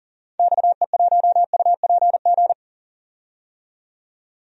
Morse code, heard as a keyed tone, about 40 wpm, 700 Hz. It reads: XE1UPG
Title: DXCC-100 Call Signs - 6 Characters - 40wpm